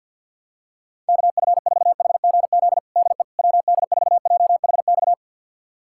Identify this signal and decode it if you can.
Morse code, heard as a keyed tone, about 49 wpm, 700 Hz. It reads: KF4HGZ DE WD4JHX
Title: USA Callsign DE Callsign - 6 Characters - 50wpm